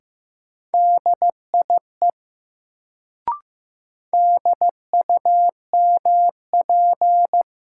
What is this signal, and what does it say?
Morse code, heard as a keyed tone, about 15 wpm, 700 Hz. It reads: DIE DUMP
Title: Sentences from Top 1300 Words - New Word Drill Alt - 15wpm